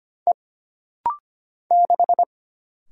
Morse code, heard as a keyed tone, about 25 wpm, 700 Hz. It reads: E 6
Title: Single Letter-Number Alt - Rapid-Fire 25wpm